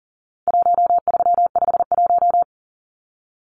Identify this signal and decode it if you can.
Morse code, heard as a keyed tone, about 40 wpm, 700 Hz. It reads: 1351
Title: Numbers - 4-Digit 40wpm